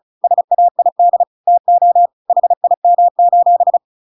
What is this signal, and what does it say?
Morse code, heard as a keyed tone, about 36 wpm, 700 Hz. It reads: SAID TO HIM:
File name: Pinocchio / Pinocchio ch8 35 35 ik1hge